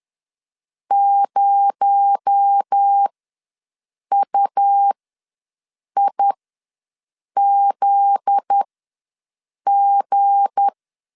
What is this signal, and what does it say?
Morse code, heard as a keyed tone, about 10 wpm, spaced as for 6 wpm, 800 Hz. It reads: 0UIZG
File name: LEZIONI / 7 GF9Z / 7 GF9Z ws 10wpm 3919